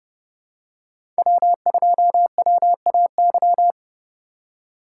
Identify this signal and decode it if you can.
Morse code, heard as a keyed tone, about 30 wpm, 700 Hz. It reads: W2WAY
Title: USA Call Signs - 5 Characters - 2x Character Spacing - 30wpm